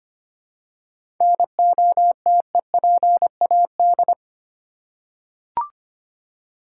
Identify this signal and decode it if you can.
Morse code, heard as a keyed tone, about 25 wpm, 700 Hz. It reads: NOTEPAD
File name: practice / 25wpm / Easy Words with Hints - 25wpm